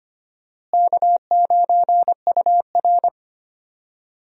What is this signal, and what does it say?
Morse code, heard as a keyed tone, about 25 wpm, 700 Hz. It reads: K9UR
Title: USA Call Signs - 4 Characters - Repeated 2x - 25wpm